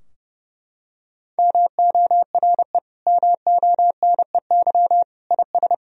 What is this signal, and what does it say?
Morse code, heard as a keyed tone, about 30 wpm, 700 Hz. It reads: MORE MONEY IS
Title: Sentences from Top 1000 Words - Review Part 3 - 30wpm